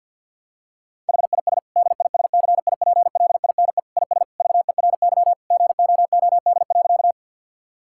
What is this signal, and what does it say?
Morse code, heard as a keyed tone, about 50 wpm, 700 Hz. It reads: HIS DISCIPLINE IS VERY GOOD.
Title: Sentences from Top 1000 Words - Review Part 6 - 50wpm